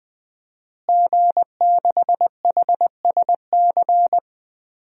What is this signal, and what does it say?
Morse code, heard as a keyed tone, about 20 wpm, 700 Hz. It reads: G6HSC
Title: DXCC-100 Call Signs - 5 Characters - Repeated 2x - 20wpm